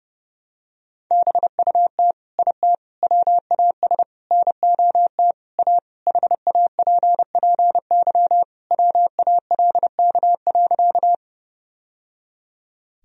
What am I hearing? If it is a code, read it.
Morse code, heard as a keyed tone, about 30 wpm, 700 Hz. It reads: BUT IT WAS NOT A HAPPY WALK.
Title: Sentences from Top 1000 Words - Review Part 1 - 30wpm